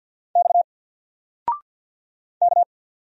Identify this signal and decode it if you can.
Morse code, heard as a keyed tone, about 50 wpm, 700 Hz. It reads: X K
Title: Single Letter-Number 50wpm